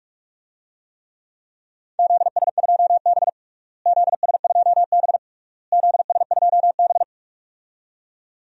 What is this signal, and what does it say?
Morse code, heard as a keyed tone, about 42 wpm, 700 Hz. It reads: ZS2B ZS2B ZS2B
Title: DXCC-100 Call Signs - 4 Characters - Repeated 3x - 45wpm